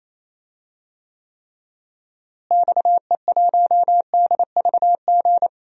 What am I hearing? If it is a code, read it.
Morse code, heard as a keyed tone, about 27 wpm, 700 Hz. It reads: XE1DVG
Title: DXCC-100 Call Signs - 6 Characters - Repeated 3x - 28wpm